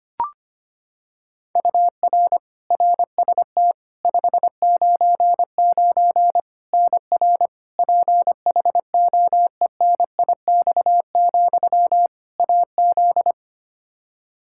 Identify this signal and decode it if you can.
Morse code, heard as a keyed tone, about 25 wpm, 700 Hz. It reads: UR RST 599 NR PHOENIX, AZ
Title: Practice QSO - Exchange 1 - Beginning - 25wpm